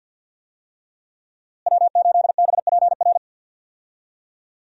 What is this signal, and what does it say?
Morse code, heard as a keyed tone, about 52 wpm, 700 Hz. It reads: W8BPR
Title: USA Call Signs - 5 Characters - Repeated 3x - 50wpm